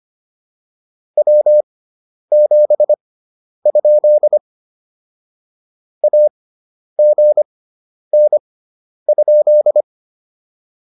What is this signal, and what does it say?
Morse code, heard as a keyed tone, about 25 wpm, spaced as for 10 wpm, 600 Hz. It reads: W7? AGN?